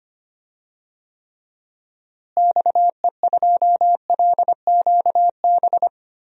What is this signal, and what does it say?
Morse code, heard as a keyed tone, about 25 wpm, 700 Hz. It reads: XE2LQB
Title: DXCC-100 Call Signs - 6 Characters - Repeated 3x - 25wpm